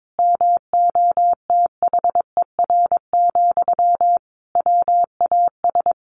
Morse code, heard as a keyed tone, about 22 wpm, 700 Hz. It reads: MOTHER, WAS
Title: Sentences from Top 1500 Words 22wpm